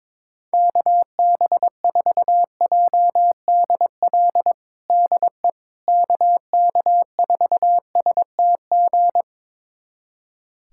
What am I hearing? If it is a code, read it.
Morse code, heard as a keyed tone, about 22 wpm, 700 Hz. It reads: KB4JDL DE KK4STG
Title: USA Callsign DE Callsign 22wpm